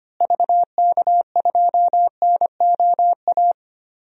Morse code, heard as a keyed tone, about 25 wpm, 700 Hz. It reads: VK2NOA
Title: DXCC-100 Call Signs - 6 Characters - Repeated 2x - 25wpm